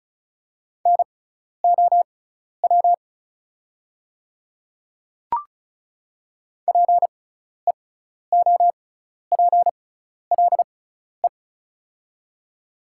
Morse code, heard as a keyed tone, about 34 wpm, spaced as for 12 wpm, 700 Hz. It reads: NOW PEOPLE
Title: 6x Character Spacing - Top 300 Words - 35wpm